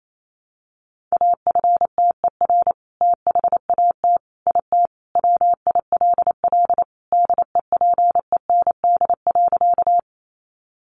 Morse code, heard as a keyed tone, about 28 wpm, 700 Hz. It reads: AFTER THAT IT WILL DEPEND.
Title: Sentences from Top 1300 Words 28wpm